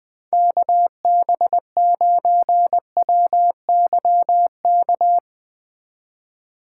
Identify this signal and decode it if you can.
Morse code, heard as a keyed tone, about 20 wpm, 700 Hz. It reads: KB9WYK
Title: Speed Racing - USA 2x3 Call Signs 17wpm